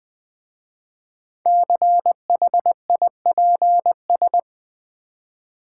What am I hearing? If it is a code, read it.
Morse code, heard as a keyed tone, about 21 wpm, 700 Hz. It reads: CHIPS
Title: Easy Words with Hints - 20wpm